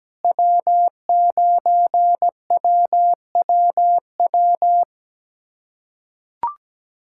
Morse code, heard as a keyed tone, about 17 wpm, 700 Hz. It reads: W9WWW
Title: USA Callsign DE Callsign - 5 Characters - 17wpm